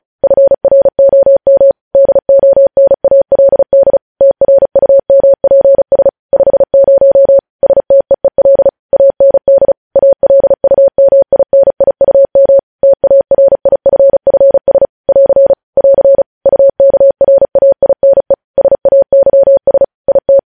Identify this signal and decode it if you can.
Morse code, heard as a keyed tone, about 36 wpm, 550 Hz. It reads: FROM DONALD TRUMPS 50 STEEL AND ALUMINIUM TARIFFS <AR> <AR> UKRAINE SAYS IT
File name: podcast / News headlines-1x-35x35